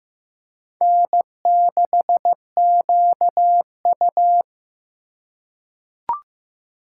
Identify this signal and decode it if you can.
Morse code, heard as a keyed tone, about 15 wpm, 700 Hz. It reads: N6QU
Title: USA Call Signs - 4 Characters - 2x Character Spacing - 15wpm